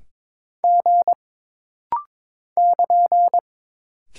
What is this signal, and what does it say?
Morse code, heard as a keyed tone, about 22 wpm, 700 Hz. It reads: G <KN>
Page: This rapid-fire timing is geared to push your ICR (Instant Character Recognition) skills.